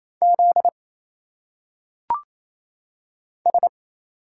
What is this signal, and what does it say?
Morse code, heard as a keyed tone, about 27 wpm, 700 Hz. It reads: Z S
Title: Single Letters - Phonetic Pronunciation 28wpm